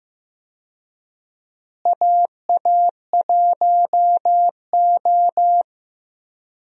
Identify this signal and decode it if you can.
Morse code, heard as a keyed tone, about 15 wpm, 700 Hz. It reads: AA1O
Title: USA Call Signs - 4 Characters - Repeated 2x - 15wpm